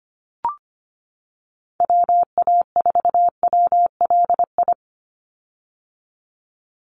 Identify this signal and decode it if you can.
Morse code, heard as a keyed tone, about 25 wpm, 700 Hz. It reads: WA4WLI
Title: Call Signs - 25wpm